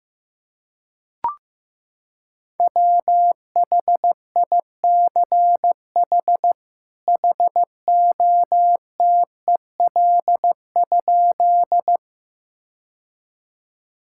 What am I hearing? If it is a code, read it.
Morse code, heard as a keyed tone, about 15 wpm, 700 Hz. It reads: WHICH HOTEL?
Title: Sentences from Top 1600 Words 15wpm